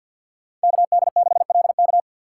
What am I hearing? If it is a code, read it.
Morse code, heard as a keyed tone, about 47 wpm, 700 Hz. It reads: KD6LK
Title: USA Call Signs - 5 Characters - 4x Character Spacing - 50wpm